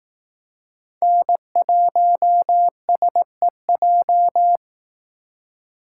Morse code, heard as a keyed tone, about 18 wpm, 700 Hz. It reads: N1SEJ
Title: Speed Racing - USA 1x3 Call Signs 15wpm